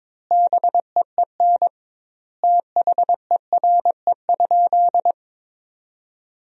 Morse code, heard as a keyed tone, about 21 wpm, 700 Hz. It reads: BEEN THERE?
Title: 2x Word-Spacing - Sentences from Top 100 Words - 22wpm